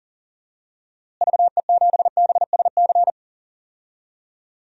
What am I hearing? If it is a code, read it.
Morse code, heard as a keyed tone, about 40 wpm, 700 Hz. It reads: VE7BSC
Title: DXCC-100 Call Signs - 6 Characters - Repeated 2x - 40wpm